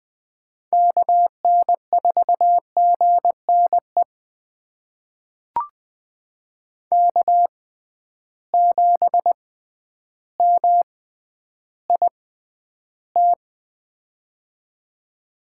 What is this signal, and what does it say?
Morse code, heard as a keyed tone, about 20 wpm, spaced as for 11 wpm, 700 Hz. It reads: KN4GNE K7MIT
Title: USA Call Signs - 6x Character Spacing - 20wpm